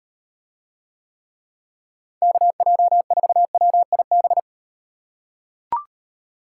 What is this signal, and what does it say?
Morse code, heard as a keyed tone, about 37 wpm, 700 Hz. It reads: KJ4WIB